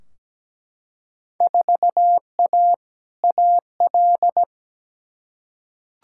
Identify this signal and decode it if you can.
Morse code, heard as a keyed tone, about 17 wpm, 700 Hz. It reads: 4A AL
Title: Field Day Exchanges - 17wpm